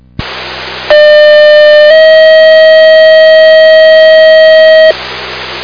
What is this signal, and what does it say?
Two-tone page, heard as a one-second tone, then a three-second tone, about 600 then 640 Hz, with noise. It is heard from 0.9 to 4.9 s.